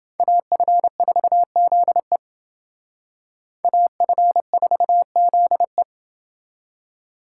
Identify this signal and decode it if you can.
Morse code, heard as a keyed tone, about 28 wpm, 700 Hz. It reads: AF4ZE AF4ZE